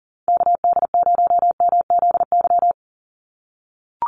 Morse code, heard as a keyed tone, about 40 wpm, 700 Hz. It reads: KD0MZY
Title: USA Call Signs 40wpm